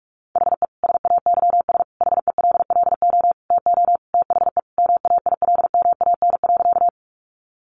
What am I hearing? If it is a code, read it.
Morse code, heard as a keyed tone, about 44 wpm, 700 Hz. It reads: HE SAYS HELLO TO THE MAILMAN.
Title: Easy Phrases 45wpm